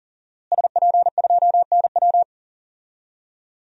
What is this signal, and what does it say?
Morse code, heard as a keyed tone, about 40 wpm, 700 Hz. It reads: SP2NW
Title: DXCC-100 Call Signs - 5 Characters - Repeated 2x - 40wpm